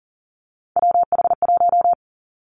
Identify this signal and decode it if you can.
Morse code, heard as a keyed tone, about 41 wpm, 690 Hz. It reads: WH1
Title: Three Challenging Letter-Number Combinations 40wpm